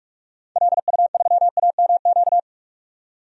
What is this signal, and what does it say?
Morse code, heard as a keyed tone, about 44 wpm, 700 Hz. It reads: LU3AMQ
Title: DXCC-100 Call Signs - 6 Characters - 45wpm